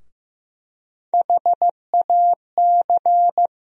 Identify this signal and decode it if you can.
Morse code, heard as a keyed tone, about 15 wpm, 700 Hz. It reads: HAC
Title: USA Call Sign Postfixes - 3 Letters 15wpm